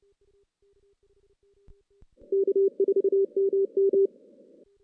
Morse code, heard as a keyed tone, about 28 wpm, 410 Hz. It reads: K4MM
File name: 2016 JAN CW NAQP / 2016 Jan naqp k5ot - 0-6